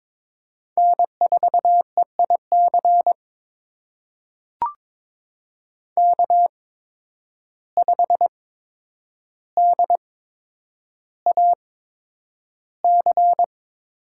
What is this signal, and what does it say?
Morse code, heard as a keyed tone, about 22 wpm, spaced as for 9 wpm, 700 Hz. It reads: N4EIC K5DAC
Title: USA Call Signs - 5 Characters - 8x Character Spacing - 22wpm